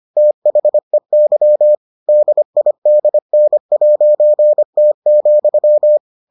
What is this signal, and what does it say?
Morse code, heard as a keyed tone, about 25 wpm, 600 Hz. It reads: THEY DIDN'T,